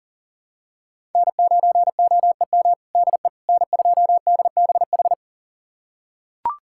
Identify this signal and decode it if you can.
Morse code, heard as a keyed tone, about 40 wpm, 700 Hz. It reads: N9OEM DE N2DBH
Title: USA Callsign DE Callsign - 5 Characters - 40wpm